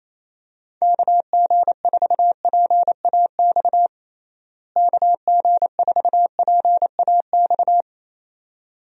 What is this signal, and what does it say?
Morse code, heard as a keyed tone, about 28 wpm, 700 Hz. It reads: KG4PAX KG4PAX
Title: USA Call Signs - 6 Characters - Repeated 2x - 28wpm